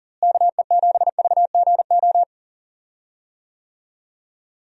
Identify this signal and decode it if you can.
Morse code, heard as a keyed tone, about 40 wpm, 700 Hz. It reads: KE7VGO
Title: USA Call Signs - 6 Characters - Repeated 3x - 40wpm